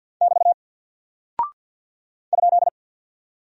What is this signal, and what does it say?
Morse code, heard as a keyed tone, about 48 wpm, 700 Hz. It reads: <BT> ?
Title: Single Character (Letter:Number:Punctuation:Prosign) Alt 50wpm